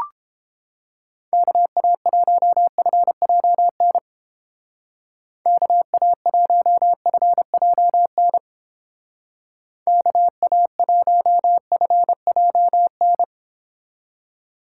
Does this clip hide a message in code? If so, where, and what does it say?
Yes: Morse code from 1.3 to 13.2 s, heard as a keyed tone, about 29 wpm, 700 Hz. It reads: KA1FJN KA1FJN KA1FJN